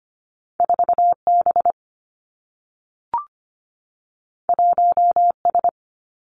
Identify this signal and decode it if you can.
Morse code, heard as a keyed tone, about 25 wpm, 700 Hz. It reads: 4B 1S
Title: Two Challenging Letter-Number Combinations 25wpm